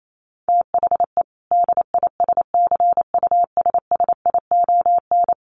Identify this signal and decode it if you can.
Morse code, heard as a keyed tone, about 28 wpm, 700 Hz. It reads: THE DISCUSSION
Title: Sentences from Top 1100 Words 28wpm